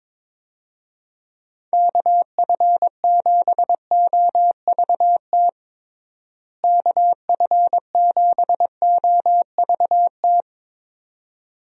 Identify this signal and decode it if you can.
Morse code, heard as a keyed tone, about 22 wpm, 700 Hz. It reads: KF7OVT KF7OVT
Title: USA Call Signs - 6 Characters - Repeated 2x - 22wpm